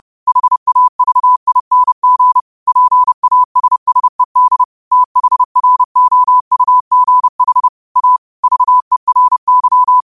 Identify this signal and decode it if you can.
Morse code, heard as a keyed tone, about 30 wpm, 1000 Hz. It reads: HAVING PASSED THROUGH A VERY
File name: qotdfiles / 30wpm